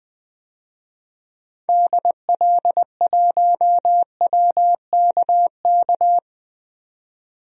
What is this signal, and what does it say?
Morse code, heard as a keyed tone, about 21 wpm, 700 Hz. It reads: DL1WKK
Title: DXCC-100 Call Signs - 6 Characters - 20wpm